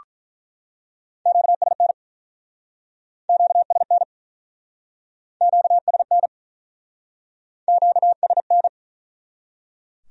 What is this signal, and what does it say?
Morse code, heard as a keyed tone, about 48 wpm, 700 Hz. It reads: QSN QSN QSN QSN